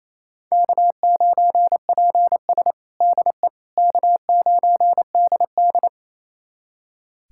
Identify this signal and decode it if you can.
Morse code, heard as a keyed tone, about 28 wpm, 700 Hz. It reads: K9PS DE K9DD
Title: USA Callsign DE Callsign - 4 Characters - 28wpm